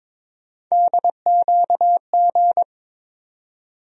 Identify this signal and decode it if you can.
Morse code, heard as a keyed tone, about 22 wpm, 700 Hz. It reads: DQG
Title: USA Call Sign Postfixes - 3 Letters 22wpm